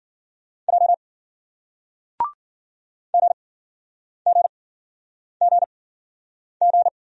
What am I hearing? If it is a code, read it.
Morse code, heard as a keyed tone, about 54 wpm, spaced as for 9 wpm, 700 Hz. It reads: J GGGG